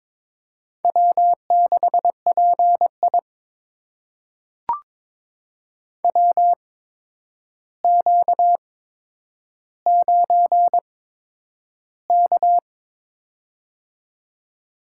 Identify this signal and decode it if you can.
Morse code, heard as a keyed tone, about 22 wpm, spaced as for 9 wpm, 700 Hz. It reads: W6PI WQ9K
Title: USA Call Signs - 4 Characters - 8x Character Spacing - 22wpm